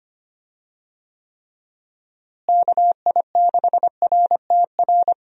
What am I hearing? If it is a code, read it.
Morse code, heard as a keyed tone, about 25 wpm, 700 Hz. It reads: KI6RTR
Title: USA Call Signs - 6 Characters - Repeated 2x - 25wpm